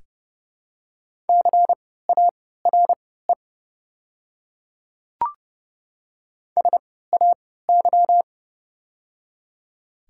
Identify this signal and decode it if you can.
Morse code, heard as a keyed tone, about 31 wpm, spaced as for 17 wpm, 690 Hz. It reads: CARE SAY
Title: 3x Character Spacing - Top 300 Words - 30wpm